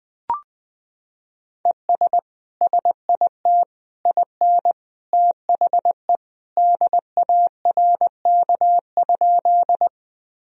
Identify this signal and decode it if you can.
Morse code, heard as a keyed tone, about 20 wpm, 700 Hz. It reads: ES SIT IN THE DARK?
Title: Sentences from Top 500 Words and Abbreviations 20wpm